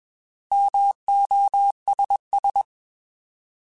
Morse code, heard as a keyed tone, about 21 wpm, 800 Hz. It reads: MOSS
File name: Alphabetical / Alphabetical mp3s / Ms @20